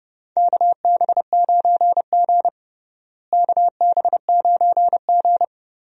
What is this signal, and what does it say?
Morse code, heard as a keyed tone, about 30 wpm, 700 Hz. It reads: KB9G KB9G
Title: USA Call Signs - 4 Characters - Repeated 2x - 30wpm